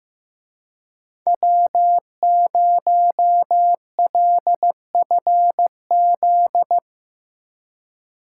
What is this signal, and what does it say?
Morse code, heard as a keyed tone, about 17 wpm, 700 Hz. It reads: W0LFZ
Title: Speed Racing - USA 1x3 Call Signs 15wpm